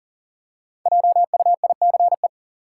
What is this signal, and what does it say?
Morse code, heard as a keyed tone, about 40 wpm, 700 Hz. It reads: JUICE
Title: Five Letter Words - 40wpm